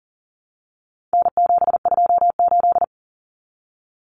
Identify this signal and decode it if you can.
Morse code, heard as a keyed tone, about 40 wpm, 700 Hz. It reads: N728
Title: Common Cut Numbers - 4-Digit 40wpm